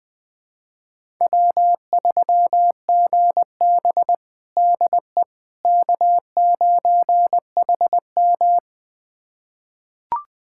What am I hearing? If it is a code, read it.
Morse code, heard as a keyed tone, about 20 wpm, 700 Hz. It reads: W3GB DE K9HM
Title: USA Callsign DE Callsign - 4 Characters - 20wpm